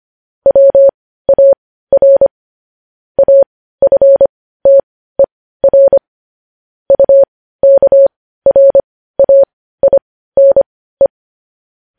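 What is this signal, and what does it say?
Morse code, heard as a keyed tone, about 25 wpm, spaced as for 15 wpm, 550 Hz. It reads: WAR AFTER UKRAINE
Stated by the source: Morse code news headlines broadcast.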